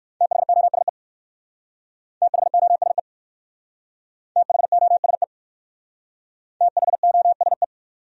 Morse code, heard as a keyed tone, about 54 wpm, 700 Hz. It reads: THOSE THOSE THOSE THOSE
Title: Speed-Racing Top 300 Words 45wpm